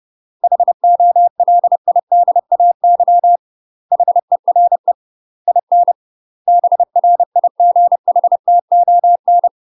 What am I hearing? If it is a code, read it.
Morse code, heard as a keyed tone, about 30 wpm, 700 Hz. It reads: HOLIDAY HERE IN BRIGHTON